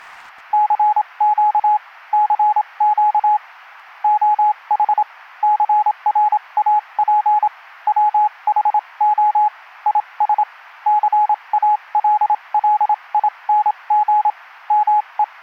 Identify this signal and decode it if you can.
Morse code, heard as a keyed tone, about 27 wpm, spaced as for 23 wpm, 840 Hz, with radio noise. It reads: CQ CQ OH CRAP WHO IS CALLING ME